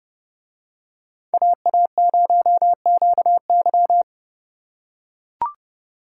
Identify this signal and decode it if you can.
Morse code, heard as a keyed tone, about 29 wpm, 700 Hz. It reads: AA0QY